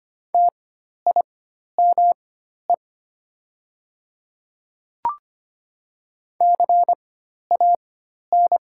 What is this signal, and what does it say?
Morse code, heard as a keyed tone, about 25 wpm, spaced as for 12 wpm, 690 Hz. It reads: TIME CAN
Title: 4x Character Spacing - Top 100 Words - 25wpm